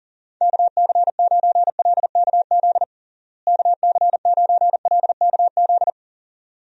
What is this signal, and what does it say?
Morse code, heard as a keyed tone, about 40 wpm, 700 Hz. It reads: KC9LKZ KC9LKZ
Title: USA Call Signs - 6 Characters - Repeated 2x - 40wpm